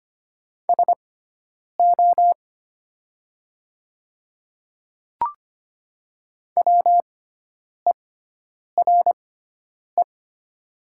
Morse code, heard as a keyed tone, about 25 wpm, spaced as for 9 wpm, 700 Hz. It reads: SO WERE
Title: 6x Character Spacing - Top 100 Words - 25wpm